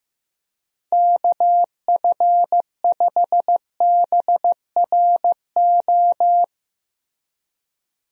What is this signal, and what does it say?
Morse code, heard as a keyed tone, about 15 wpm, 700 Hz. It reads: KF5BRO
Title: USA Call Signs - 6 Characters - Repeated 3x - 15wpm